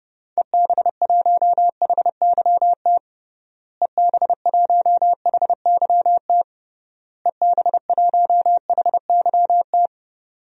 Morse code, heard as a keyed tone, about 30 wpm, 700 Hz. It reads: EB1HYT EB1HYT EB1HYT
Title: DXCC-100 Call Signs - 6 Characters - Repeated 3x - 30wpm